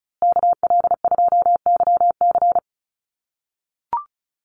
Morse code, heard as a keyed tone, about 34 wpm, 700 Hz. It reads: KL2YC
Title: USA Call Signs Repeated 3x 35wpm